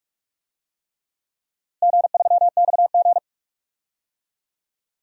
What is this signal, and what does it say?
Morse code, heard as a keyed tone, about 46 wpm, 700 Hz. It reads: G3XG
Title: DXCC-100 Call Signs - 4 Characters - Repeated 3x - 45wpm